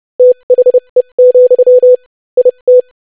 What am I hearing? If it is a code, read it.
Morse code, heard as a keyed tone, about 30 wpm, spaced as for 25 wpm, 500 Hz. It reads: THE, IT
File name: lesson / basicphrase-25wpm